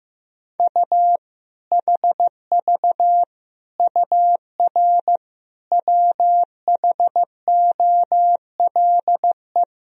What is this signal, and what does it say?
Morse code, heard as a keyed tone, about 15 wpm, 700 Hz. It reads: U HV UR WHOLE
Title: Sentences from Top 400 Words and Abbreviations 15wpm